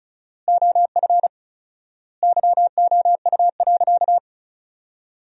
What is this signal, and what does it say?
Morse code, heard as a keyed tone, about 34 wpm, 700 Hz. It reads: OF YOU.
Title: 4x Word-Spacing - Sentences from Top 400 Words - 35wpm